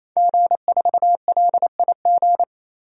Morse code, heard as a keyed tone, about 28 wpm, 700 Hz. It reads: G4LIG
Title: DXCC-100 Call Signs - 5 Characters - 28wpm